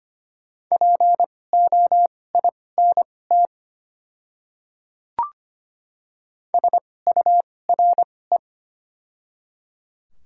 Morse code, heard as a keyed tone, about 25 wpm, spaced as for 18 wpm, 690 Hz. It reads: POINT SURE
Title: 2x Character Spacing - Top 300 Words - 25wpm